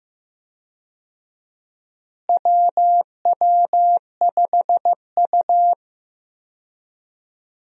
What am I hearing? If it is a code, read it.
Morse code, heard as a keyed tone, about 15 wpm, 700 Hz. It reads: WW5U
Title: USA Call Signs - 4 Characters - Repeated 2x - 15wpm